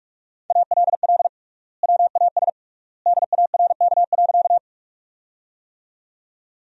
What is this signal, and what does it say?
Morse code, heard as a keyed tone, about 41 wpm, 700 Hz. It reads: ALL WAS DARK.
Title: 3x Word-Spacing - Sentences from Top 500 Words - 45wpm